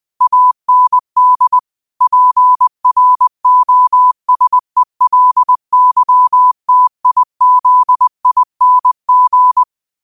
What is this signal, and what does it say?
Morse code, heard as a keyed tone, about 20 wpm, 1000 Hz. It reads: AND PROSELYTIZING
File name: qotdfiles / 20wpm